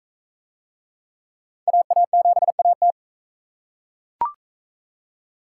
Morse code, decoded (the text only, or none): AA7AT